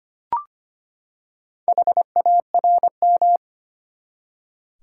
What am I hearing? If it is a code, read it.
Morse code, heard as a keyed tone, about 25 wpm, 700 Hz. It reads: HARM
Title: Four Letter Words - 25wpm